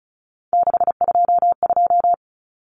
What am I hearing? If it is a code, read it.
Morse code, heard as a keyed tone, about 34 wpm, 700 Hz. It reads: 622
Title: Common Cut Numbers - 3-Digit 35wpm